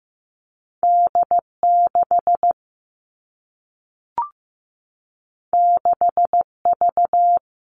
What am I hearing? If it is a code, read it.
Morse code, heard as a keyed tone, about 15 wpm, 700 Hz. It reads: D6 6V
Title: All Cut Numbers - 2-Digit 15wpm